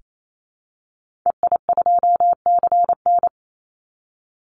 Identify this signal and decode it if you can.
Morse code, heard as a keyed tone, about 28 wpm, 700 Hz. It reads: EI2CN
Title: Call Signs Repeated 3x - 28wpm